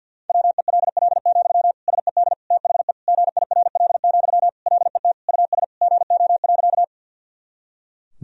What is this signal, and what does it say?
Morse code, heard as a keyed tone, about 51 wpm, 700 Hz. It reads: WELL, SED THE GIRL, LET US GO.
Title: Sentences from Top 400 Words and Abbreviations 50wpm